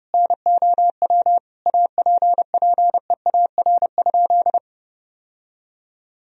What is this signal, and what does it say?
Morse code, heard as a keyed tone, about 32 wpm, 700 Hz. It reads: NOW APPEAR?
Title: Sentences from Top 1000 Words - Review Part 2 - 30wpm